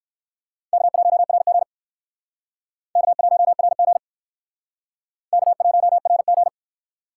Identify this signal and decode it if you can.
Morse code, heard as a keyed tone, about 56 wpm, 700 Hz. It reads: K1RG K1RG K1RG
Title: Speed Racing - USA Call Signs 45wpm